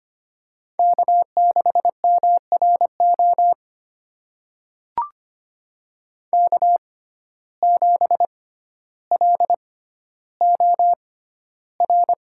Morse code, heard as a keyed tone, about 25 wpm, spaced as for 13 wpm, 700 Hz. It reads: K6MRO K7LOR
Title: USA Call Signs - 5 Characters - 6x Character Spacing - 25wpm